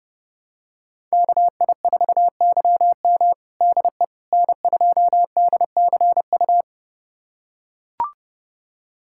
Morse code, heard as a keyed tone, about 32 wpm, 700 Hz. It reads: KI4YM DE N2DCU
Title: USA Callsign DE Callsign - 5 Characters - 30wpm